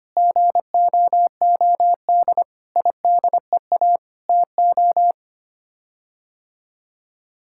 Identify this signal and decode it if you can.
Morse code, heard as a keyed tone, about 25 wpm, 700 Hz. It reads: GOOD IDEA TO
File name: Sets of 9 Words - 25wpm